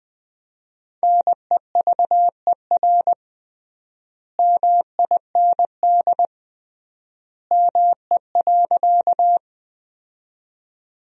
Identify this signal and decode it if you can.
Morse code, heard as a keyed tone, about 20 wpm, 700 Hz. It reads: NEVER MIND ME.
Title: 3x Word-Spacing - Sentences from Top 500 Words - 20wpm